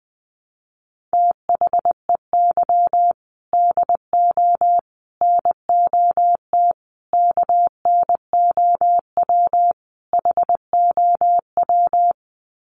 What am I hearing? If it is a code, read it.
Morse code, heard as a keyed tone, about 20 wpm, 700 Hz. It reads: THEY DO NOT KNOW HOW
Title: Sets of 5 Words Encore2 20wpm